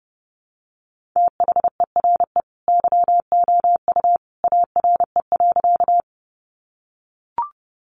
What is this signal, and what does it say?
Morse code, heard as a keyed tone, about 30 wpm, 700 Hz. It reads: THERE YOU ARE.
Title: Sentences from Top 100 Words 30wpm